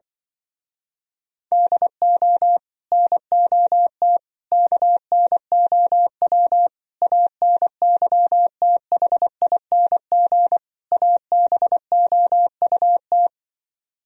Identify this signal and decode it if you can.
Morse code, heard as a keyed tone, about 24 wpm, 700 Hz. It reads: DO NOT KNOW ANYTHING ABOUT
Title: Speed-Racing - Sets of 5 Words 20wpm